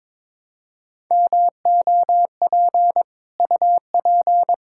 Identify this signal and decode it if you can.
Morse code, heard as a keyed tone, about 21 wpm, 690 Hz. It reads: MOP UP
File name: practice / 22wpm / Easy Sentences with Hints - 22wpm